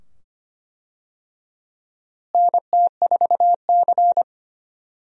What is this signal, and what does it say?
Morse code, heard as a keyed tone, about 25 wpm, 700 Hz. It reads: NT4C
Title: USA Call Signs - 4 Characters - Repeated 2x - 25wpm